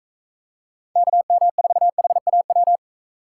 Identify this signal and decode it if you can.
Morse code, heard as a keyed tone, about 43 wpm, 700 Hz. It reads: KM4HAW